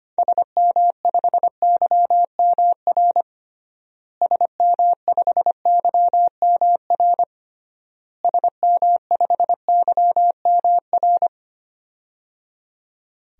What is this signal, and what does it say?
Morse code, heard as a keyed tone, about 25 wpm, 700 Hz. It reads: SM5YMR SM5YMR SM5YMR
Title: DXCC-100 Call Signs - 6 Characters - Repeated 3x - 25wpm